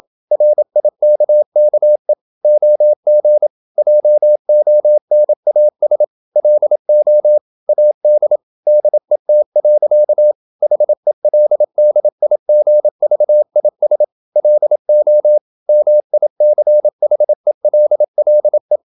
Morse code, heard as a keyed tone, about 27 wpm, 600 Hz. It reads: RIKKE OG JONAS LO AD DET. HELDIGVIS LO MICHELLE
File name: Ferie QRQ 27wpm